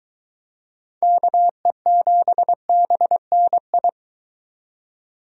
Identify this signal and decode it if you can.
Morse code, heard as a keyed tone, about 23 wpm, 700 Hz. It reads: KE7BNI